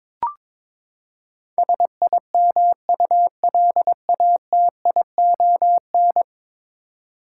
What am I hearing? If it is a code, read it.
Morse code, heard as a keyed tone, about 22 wpm, 700 Hz. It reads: SIMULATION
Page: morse-code-ninja-practice-22wpm